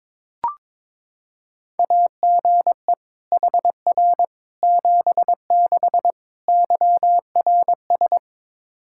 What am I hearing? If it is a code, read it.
Morse code, heard as a keyed tone, about 22 wpm, 700 Hz. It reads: AGE HR 76 YRS
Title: Practice QSO - Exchange 3 - More Info - 22wpm